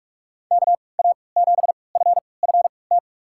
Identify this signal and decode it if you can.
Morse code, heard as a keyed tone, about 46 wpm, spaced as for 26 wpm, 700 Hz. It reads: KA7FFT
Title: USA Call Signs - 6 Characters - 3x Character Spacing - 45wpm